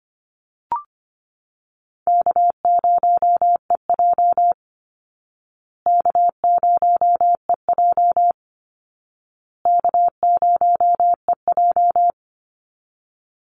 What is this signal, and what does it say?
Morse code, heard as a keyed tone, about 25 wpm, 700 Hz. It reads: K0EJ K0EJ K0EJ
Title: Easy Call Signs Repeated 3x - 25wpm